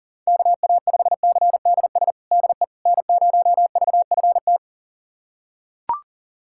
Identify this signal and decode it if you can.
Morse code, heard as a keyed tone, about 40 wpm, 700 Hz. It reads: KA5CDS DE N0VFT
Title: USA Callsign DE Callsign 40wpm